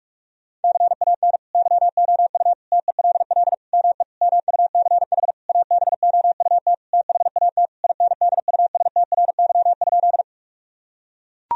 Morse code, heard as a keyed tone, about 41 wpm, 700 Hz. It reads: CAN YOU TELL ME MUCH ABOUT THAT INDUSTRY?
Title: Sentences from Top 1000 Words - Review Part 4 - 45wpm